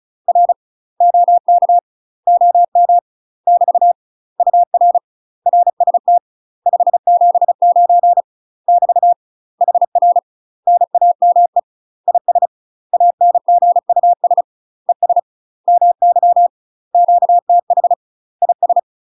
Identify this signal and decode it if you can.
Morse code, heard as a keyed tone, about 35 wpm, 700 Hz. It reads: R OK OM <BT> UR RST 579 <BT> HR NAME IS ANGUS ES MY QTH IS
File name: UK MASTER QSOs TRAINING / 35WPM / UK MASTER QSOs 0042